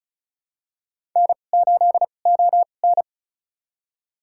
Morse code, heard as a keyed tone, about 36 wpm, spaced as for 25 wpm, 700 Hz. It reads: N8ON
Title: USA Call Signs - 4 Characters - 2x Character Spacing - 35wpm